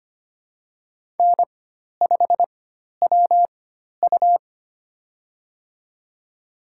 Morse code, heard as a keyed tone, about 25 wpm, spaced as for 12 wpm, 700 Hz. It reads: N5WU
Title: USA Call Signs - 4 Characters - 4x Character Spacing - 25wpm